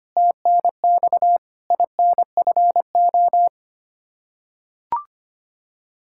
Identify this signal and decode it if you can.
Morse code, heard as a keyed tone, about 25 wpm, 700 Hz. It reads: TNX INFO
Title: Practice QSO - Exchange 3 - More Info - 25wpm